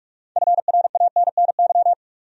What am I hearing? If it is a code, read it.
Morse code, heard as a keyed tone, about 46 wpm, 700 Hz. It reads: FRANNY
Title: USA Common Names - 45wpm